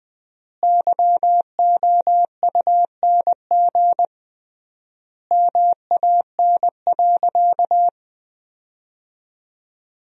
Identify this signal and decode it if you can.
Morse code, heard as a keyed tone, about 20 wpm, 700 Hz. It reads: YOUNG MAN.
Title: 3x Word-Spacing - Sentences from Top 400 Words - 20wpm